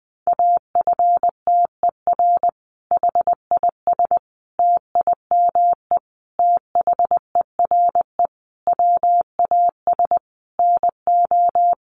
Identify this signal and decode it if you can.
Morse code, heard as a keyed tone, about 20 wpm, 700 Hz. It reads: AFTER HIS TIME THERE WAS NO
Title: Sentences from Top 1400 Words 20wpm